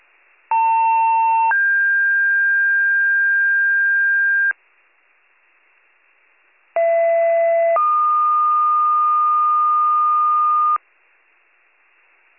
Two-tone page, heard as a one-second tone, then a three-second tone, about 900 then 1600 Hz, about 670 then 1220 Hz. It is heard from 0.5 to 10.8 s.